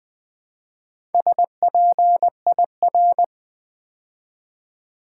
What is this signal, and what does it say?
Morse code, heard as a keyed tone, about 21 wpm, 700 Hz. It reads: SPIR
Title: Build A Word - 2 Letters At A Time - 6-Letter Words - 20wpm